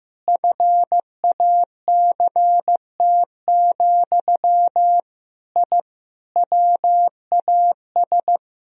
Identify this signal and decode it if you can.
Morse code, heard as a keyed tone, about 15 wpm, 700 Hz. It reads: FACT, I WAS
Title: Sentences from Top 1000 Words - Review Part 4 - 15wpm